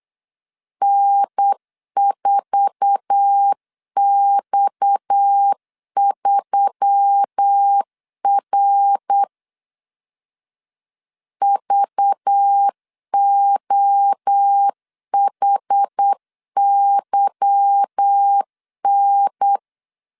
Morse code, heard as a keyed tone, about 8 wpm, 800 Hz. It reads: N4X3R VOHYN
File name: LEZIONI / 8 YJ45 / 8 YJ45 8wpm 3214